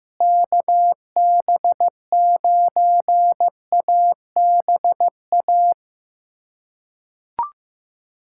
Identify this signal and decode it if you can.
Morse code, heard as a keyed tone, about 15 wpm, 700 Hz. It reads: KB9ABA
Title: USA Callsign DE Callsign - 6 Characters - 15wpm